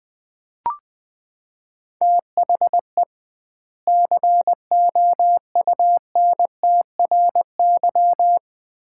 Morse code, heard as a keyed tone, about 20 wpm, 700 Hz. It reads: THE COUNTRY
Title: 2x Word-Spacing - Sets of 2 Words - 20wpm